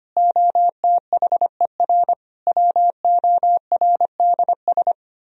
Morse code, heard as a keyed tone, about 29 wpm, 700 Hz. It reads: OTHER WORDS
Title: Speed-Racing - Sets of 3 Words 25wpm